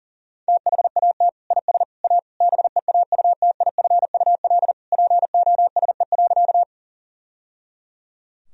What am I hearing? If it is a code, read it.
Morse code, heard as a keyed tone, about 40 wpm, 700 Hz. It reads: THAT IS A BEAUTIFUL POSE.
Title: Sentences from Top 1000 Words - Review Part 2 - 40wpm